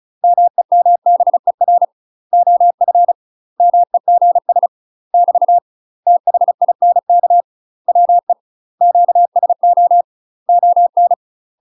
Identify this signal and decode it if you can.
Morse code, heard as a keyed tone, about 34 wpm, 700 Hz. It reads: MEMBER OF MEGS <BT> THINK WE QSO ON
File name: UK MASTER QSOs TRAINING / 35WPM / UK MASTER QSOs 0034